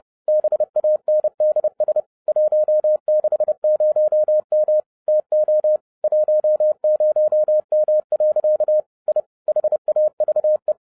Morse code, heard as a keyed tone, about 30 wpm, 610 Hz. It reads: BANDS 160M TO 10M. I HAVE
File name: 2011-07-mp3 / n20 GREETINGS FROM SINGAPORE 30wpm